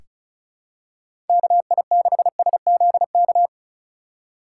Morse code, heard as a keyed tone, about 36 wpm, 700 Hz. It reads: KI6SZK
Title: USA Call Signs - 6 Characters - 4x Character Spacing - 35wpm